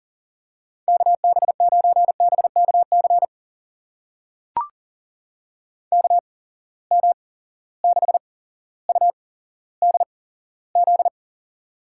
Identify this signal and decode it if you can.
Morse code, heard as a keyed tone, about 40 wpm, spaced as for 17 wpm, 700 Hz. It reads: KB9BKC KM6UDZ
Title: USA Call Signs - 6 Characters - 8x Character Spacing - 40wpm